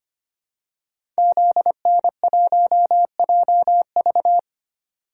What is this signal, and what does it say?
Morse code, heard as a keyed tone, about 25 wpm, 700 Hz. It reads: ZN1JV
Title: Code Groups - 5 Letter:Numbers - 25wpm